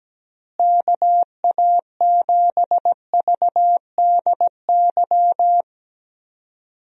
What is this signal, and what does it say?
Morse code, heard as a keyed tone, about 17 wpm, 700 Hz. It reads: KA7VDY
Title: USA Call Signs - 6 Characters - Repeated 2x - 17wpm